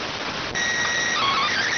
SSTV (slow-scan television), PD 290.